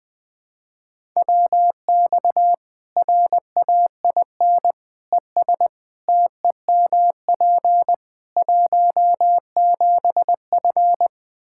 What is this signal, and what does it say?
Morse code, heard as a keyed tone, about 21 wpm, 700 Hz. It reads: WX RAIN ES TEMP 17F
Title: Practice QSO - Exchange 2 - Info - 20wpm